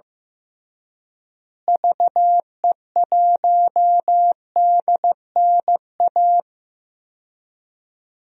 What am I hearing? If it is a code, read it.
Morse code, heard as a keyed tone, about 15 wpm, 700 Hz. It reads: VE1DNA
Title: DXCC-100 Call Signs - 6 Characters - Repeated 3x - 15wpm